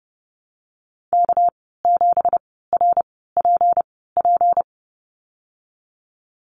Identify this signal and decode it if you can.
Morse code, heard as a keyed tone, about 32 wpm, spaced as for 17 wpm, 700 Hz. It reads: K7RPP
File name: Lesson 37g - Letter x - Call Signs 30wpm